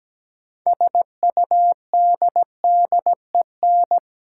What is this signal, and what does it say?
Morse code, heard as a keyed tone, about 17 wpm, 700 Hz. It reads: SUDDEN
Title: Six Letter Words - 17wpm